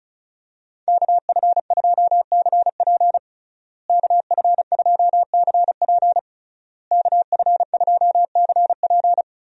Morse code, heard as a keyed tone, about 34 wpm, 700 Hz. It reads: KF2CP KF2CP KF2CP
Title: USA Call Signs - 5 Characters - Repeated 3x - 35wpm